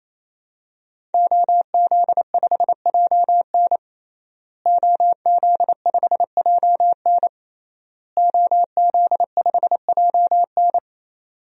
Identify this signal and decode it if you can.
Morse code, heard as a keyed tone, about 28 wpm, 700 Hz. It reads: OZ5JN OZ5JN OZ5JN
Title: DXCC-100 Call Signs - 5 Characters - Repeated 3x - 28wpm